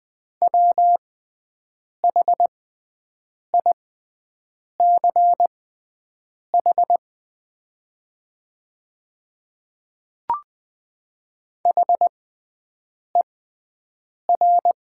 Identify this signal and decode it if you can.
Morse code, heard as a keyed tone, about 20 wpm, spaced as for 7 wpm, 700 Hz. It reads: WHICH HER
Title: 6x Character Spacing - Top 200 Words - 20wpm